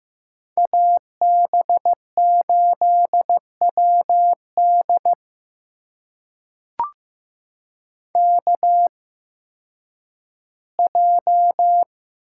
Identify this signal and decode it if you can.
Morse code, heard as a keyed tone, about 15 wpm, spaced as for 8 wpm, 700 Hz. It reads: AB8WDK J
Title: USA Call Signs - 8x Character Spacing - 15wpm